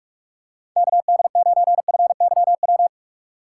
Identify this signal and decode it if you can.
Morse code, heard as a keyed tone, about 45 wpm, 700 Hz. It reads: KD9FYW
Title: USA Call Signs - 6 Characters - 3x Character Spacing - 45wpm